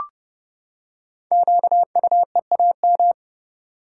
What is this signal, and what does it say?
Morse code, heard as a keyed tone, about 29 wpm, 700 Hz. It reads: QUEAM
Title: Code Groups - 5 Letters - 30wpm